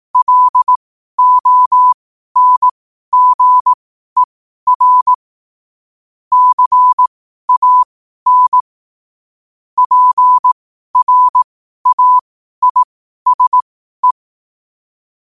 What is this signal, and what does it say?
Morse code, heard as a keyed tone, about 18 wpm, spaced as for 12 wpm, 1000 Hz. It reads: LONGER CAN PRAISE
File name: qotdfiles / 12wpm